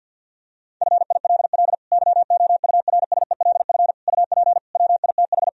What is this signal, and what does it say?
Morse code, heard as a keyed tone, about 52 wpm, 700 Hz. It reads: FILL YOURSELF UP WITH
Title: Sentences from Top 1000 Words - Review Part 3 - 50wpm